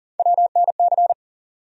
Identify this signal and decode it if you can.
Morse code, heard as a keyed tone, about 40 wpm, 700 Hz. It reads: WNC
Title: USA Call Sign Postfixes - 3 Letters 40wpm